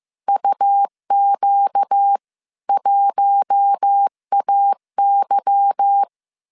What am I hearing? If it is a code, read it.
Morse code, heard as a keyed tone, about 15 wpm, 800 Hz. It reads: UQ1AY
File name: CW / DET / 1 9 1500C S C 14wpm 3245